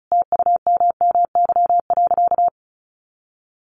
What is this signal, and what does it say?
Morse code, heard as a keyed tone, about 34 wpm, 700 Hz. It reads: TUMMY.
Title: Easy Phrases 35wpm